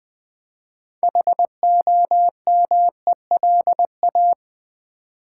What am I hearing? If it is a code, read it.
Morse code, heard as a keyed tone, about 20 wpm, 700 Hz. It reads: HOMELA
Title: Build A Word - 2 Letters At A Time - 8-Letter Words - 20wpm